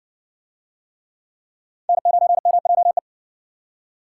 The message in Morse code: N9MJE

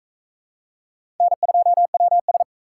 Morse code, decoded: N2WS

